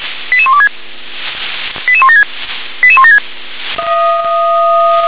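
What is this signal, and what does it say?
Five-tone selective call (ZVEI), heard as quick runs of five tones, tones of about 70 ms. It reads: 88126 88176 88176
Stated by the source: Five-tone ZVEI selective call received on an SDR.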